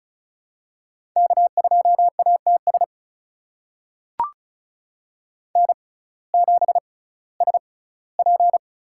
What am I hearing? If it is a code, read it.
Morse code, heard as a keyed tone, about 36 wpm, spaced as for 19 wpm, 700 Hz. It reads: K2ATS N7SP
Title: USA Call Signs - 5 Characters - 6x Character Spacing - 35wpm